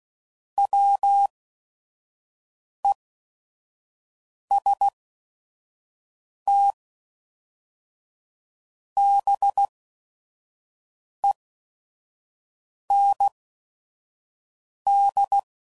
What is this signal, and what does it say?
Morse code, heard as a keyed tone, about 16 wpm, spaced as for 5 wpm, 800 Hz. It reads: WESTBEND
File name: CW Code Groups @5WPM / fromSM's / Cities